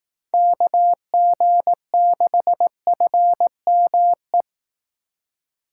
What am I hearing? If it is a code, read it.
Morse code, heard as a keyed tone, about 18 wpm, 700 Hz. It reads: KG6FME
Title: Speed Racing - USA 2x3 Call Signs 15wpm